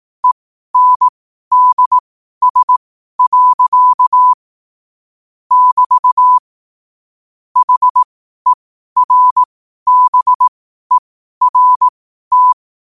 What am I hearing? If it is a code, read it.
Morse code, heard as a keyed tone, about 18 wpm, spaced as for 12 wpm, 1000 Hz. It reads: ENDS. <BT> HERBERT
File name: qotdfiles / 12wpm